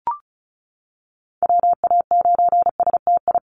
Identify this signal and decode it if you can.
Morse code, heard as a keyed tone, about 34 wpm, 700 Hz. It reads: WA9STI
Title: NA SOTA Chaser Callsigns 35wpm